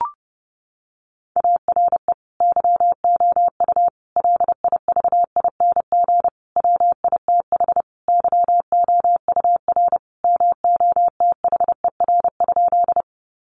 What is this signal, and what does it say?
Morse code, heard as a keyed tone, about 28 wpm, 700 Hz. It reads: ARE YOU LIVING WITH YOUR MOTHER?
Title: Sentences from Top 1100 Words 30wpm